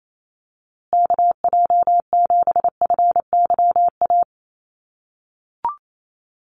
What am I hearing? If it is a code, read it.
Morse code, heard as a keyed tone, about 28 wpm, 700 Hz. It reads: KJ7FYA
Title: USA Call Signs Repeated 2x 28wpm